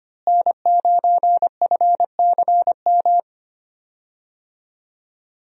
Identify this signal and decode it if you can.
Morse code, heard as a keyed tone, about 25 wpm, 700 Hz. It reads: N9FCM
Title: USA Call Signs - 5 Characters - Repeated 2x - 25wpm